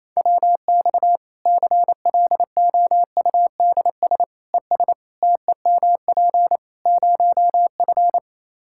Morse code, heard as a keyed tone, about 28 wpm, 700 Hz. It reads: WX CLOUDS ES TEMP 0F
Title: Practice QSO - Exchange 2 - Info - 28wpm